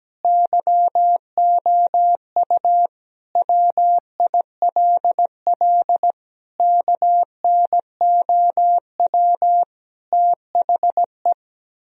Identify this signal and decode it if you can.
Morse code, heard as a keyed tone, about 17 wpm, 700 Hz. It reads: YOU WILL KNOW THE
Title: Sentences from Top 1000 Words - Review Part 3 - 17wpm